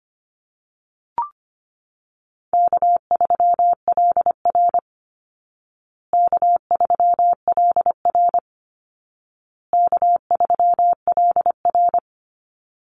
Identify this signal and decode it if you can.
Morse code, heard as a keyed tone, about 25 wpm, 700 Hz. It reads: K3LR K3LR K3LR
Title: Easy Call Signs Repeated 3x - 25wpm